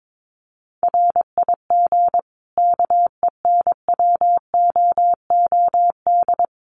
Morse code, heard as a keyed tone, about 22 wpm, 700 Hz. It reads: RIG KENWOOD
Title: Common QSO Phrases 22wpm